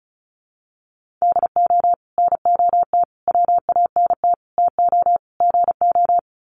Morse code, heard as a keyed tone, about 34 wpm, 700 Hz. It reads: DO NOT WANT TO GO
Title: Sets of 5 Words 35wpm